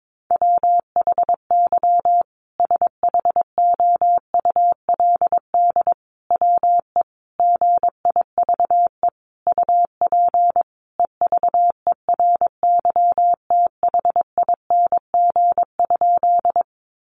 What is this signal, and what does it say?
Morse code, heard as a keyed tone, about 22 wpm, 700 Hz. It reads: WHY SHOULD WE GIVE UP EVERYTHING?
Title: Sentences from Top 700 Words 22wpm